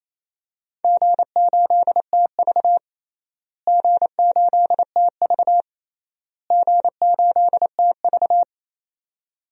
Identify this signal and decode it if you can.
Morse code, heard as a keyed tone, about 28 wpm, 700 Hz. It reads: G8TV G8TV G8TV
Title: DXCC-100 Call Signs - 4 Characters - Repeated 3x - 28wpm